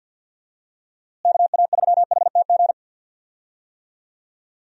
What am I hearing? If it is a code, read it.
Morse code, heard as a keyed tone, about 49 wpm, 700 Hz. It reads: KA3HTG
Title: USA Call Signs - 6 Characters - Repeated 3x - 50wpm